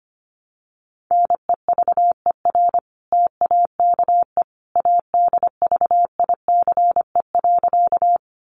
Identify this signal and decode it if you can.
Morse code, heard as a keyed tone, about 25 wpm, 700 Hz. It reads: NEVER TAKE ADVICE.
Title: Sentences from Top 1100 Words 25wpm